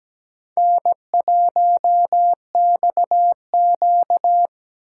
Morse code, heard as a keyed tone, about 17 wpm, 700 Hz. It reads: N1XQ
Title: USA Call Signs - 4 Characters - 3x Character Spacing - 17wpm